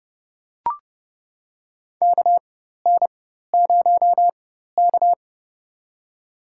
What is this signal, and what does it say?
Morse code, heard as a keyed tone, about 30 wpm, spaced as for 14 wpm, 700 Hz. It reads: KN0K
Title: USA Call Signs - 4x Character Spacing - 30wpm